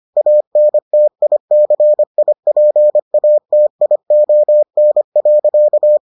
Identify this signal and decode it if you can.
Morse code, heard as a keyed tone, about 25 wpm, 600 Hz. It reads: ANTICIPATION.